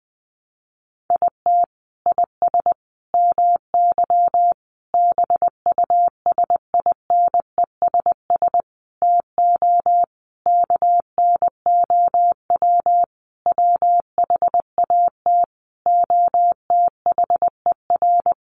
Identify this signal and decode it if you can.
Morse code, heard as a keyed tone, about 20 wpm, 700 Hz. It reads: IT IS MY BUSINESS TO KNOW WHAT OTHER
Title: Sentences from Top 600 Words 20wpm